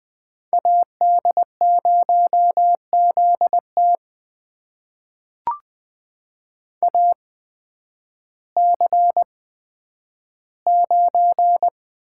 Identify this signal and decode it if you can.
Morse code, heard as a keyed tone, about 20 wpm, spaced as for 11 wpm, 700 Hz. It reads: AD0ZT AC9
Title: USA Call Signs - 5 Characters - 8x Character Spacing - 20wpm